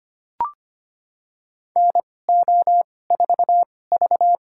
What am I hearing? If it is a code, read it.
Morse code, heard as a keyed tone, about 25 wpm, spaced as for 18 wpm, 700 Hz. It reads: NO4V
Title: USA Call Signs - 2x Character Spacing - 25wpm